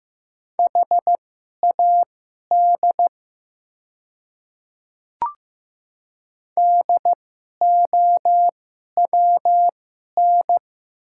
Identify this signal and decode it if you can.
Morse code, heard as a keyed tone, about 15 wpm, spaced as for 11 wpm, 700 Hz. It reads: HAD DOWN
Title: 2x Character Spacing - Top 100 Words - 15wpm